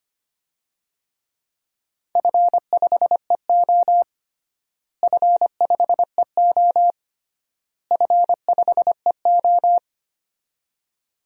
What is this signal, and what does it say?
Morse code, heard as a keyed tone, about 25 wpm, 700 Hz. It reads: F5EO F5EO F5EO
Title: DXCC-100 Call Signs - 4 Characters - Repeated 3x - 25wpm